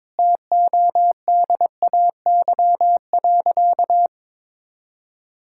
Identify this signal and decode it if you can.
Morse code, heard as a keyed tone, about 22 wpm, 700 Hz. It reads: TODAY.
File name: practice / 22wpm / Sentences with 8-Word Phrases - 22wpm